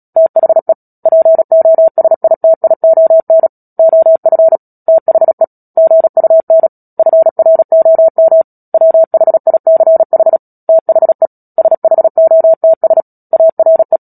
Morse code, heard as a keyed tone, about 37 wpm, 650 Hz. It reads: THE POSITION OF THE GUN FROM WHICH THE SHOTS ARE
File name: Book Two / 10-Chapter 10 - Epilogue - 36 WPM